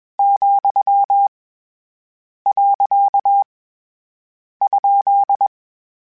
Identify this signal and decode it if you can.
Morse code, heard as a keyed tone, about 21 wpm, 800 Hz. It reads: , . ?
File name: Misc / Misc-mp3s / Words @20